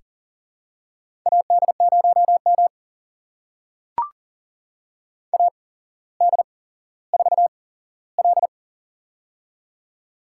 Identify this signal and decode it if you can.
Morse code, heard as a keyed tone, about 42 wpm, spaced as for 17 wpm, 700 Hz. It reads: AD0M AD4L
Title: USA Call Signs - 4 Characters - 8x Character Spacing - 40wpm